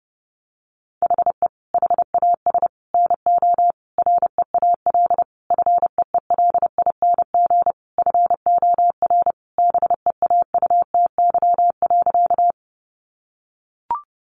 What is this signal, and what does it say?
Morse code, heard as a keyed tone, about 30 wpm, 700 Hz. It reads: HE HAS NO REAL FEELING FOR BEAUTY.
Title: Sentences from Top 1200 Words 30wpm